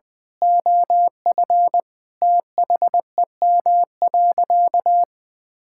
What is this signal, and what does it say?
Morse code, heard as a keyed tone, about 20 wpm, 700 Hz. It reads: OF THEM.
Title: Sentences from Top 1000 Words - Review Part 6 - 20wpm